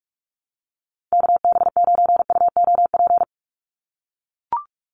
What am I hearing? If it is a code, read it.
Morse code, heard as a keyed tone, about 44 wpm, 700 Hz. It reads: KB9UOP
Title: USA Call Signs 45wpm